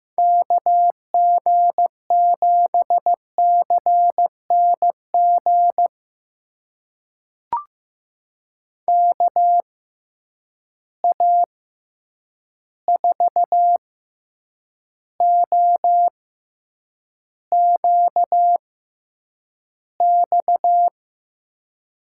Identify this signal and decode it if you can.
Morse code, heard as a keyed tone, about 15 wpm, spaced as for 8 wpm, 700 Hz. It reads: KG7CNG KA4OQX
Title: USA Call Signs - 6 Characters - 6x Character Spacing - 15wpm